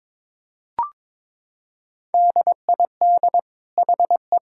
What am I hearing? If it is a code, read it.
Morse code, heard as a keyed tone, about 22 wpm, 700 Hz. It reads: DID HE
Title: Sentences from Top 1000 Words - Review Part 5 - 22wpm